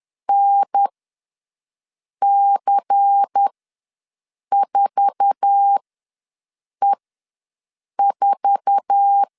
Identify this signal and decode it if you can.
Morse code, heard as a keyed tone, about 10 wpm, spaced as for 6 wpm, 800 Hz. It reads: NC4E4
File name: LEZIONI / 8 YJ45 / 8 YJ4 ws 10wpm 4020